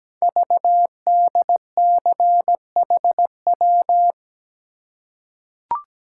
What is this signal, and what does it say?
Morse code, heard as a keyed tone, about 17 wpm, 700 Hz. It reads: VDCHW
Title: Code Groups - 5 Letters - 17wpm